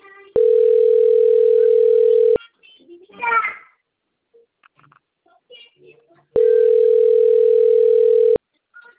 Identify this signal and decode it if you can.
Ringback tone (North America), 440 + 480 Hz, with speech.
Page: Itatiba